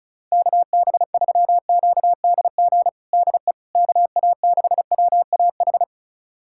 Morse code, heard as a keyed tone, about 34 wpm, 690 Hz. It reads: KB3QDG DE KA6WAH
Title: USA Callsign DE Callsign - 6 Characters - 35wpm